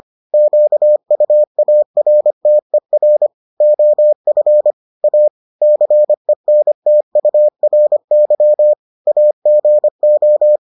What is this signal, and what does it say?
Morse code, heard as a keyed tone, about 25 wpm, 600 Hz. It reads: QUARTER OF A CENTURY AGO